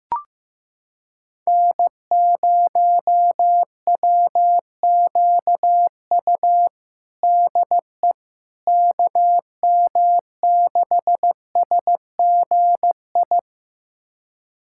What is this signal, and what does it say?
Morse code, heard as a keyed tone, about 15 wpm, 700 Hz. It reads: N0WQU DE KM6SGI
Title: USA Callsign DE Callsign 15wpm